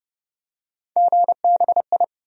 Morse code, heard as a keyed tone, about 31 wpm, 700 Hz. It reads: GBI
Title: USA Call Sign Postfixes - 3 Letters 30wpm